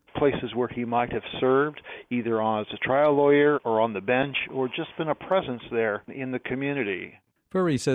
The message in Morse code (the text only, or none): none